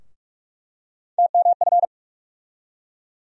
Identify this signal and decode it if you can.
Morse code, heard as a keyed tone, about 47 wpm, 700 Hz. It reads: TMF
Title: USA Call Sign Postfixes - 3 Letters 45wpm